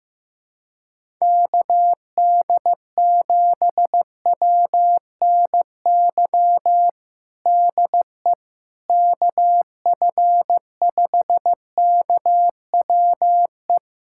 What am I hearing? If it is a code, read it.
Morse code, heard as a keyed tone, about 15 wpm, 700 Hz. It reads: KD7WNY DE KF5KWE
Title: USA Callsign DE Callsign - 6 Characters - 15wpm